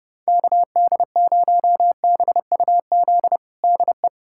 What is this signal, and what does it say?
Morse code, heard as a keyed tone, about 30 wpm, 700 Hz. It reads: KD0BUZ DE
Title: USA Callsign DE Callsign - 6 Characters - 30wpm